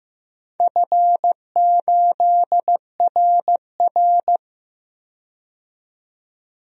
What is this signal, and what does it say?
Morse code, heard as a keyed tone, about 15 wpm, 700 Hz. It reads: F8RR
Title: DXCC-100 Call Signs - 4 Characters - 15wpm